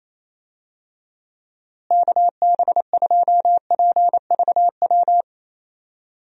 Morse code, heard as a keyed tone, about 27 wpm, 700 Hz. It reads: KB2PVW
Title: USA Call Signs - 6 Characters - Repeated 2x - 28wpm